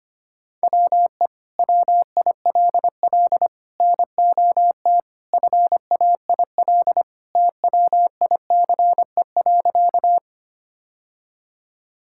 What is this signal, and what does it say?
Morse code, heard as a keyed tone, about 25 wpm, 700 Hz. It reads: WE WILL NOT FAIL TWICE.
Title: Sentences from Top 1600 Words 25wpm